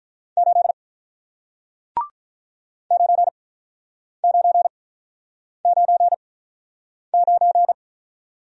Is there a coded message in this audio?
Yes: Morse code from 0.4 to 7.7 s, heard as a keyed tone, about 47 wpm, spaced as for 9 wpm, 700 Hz. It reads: 8 9999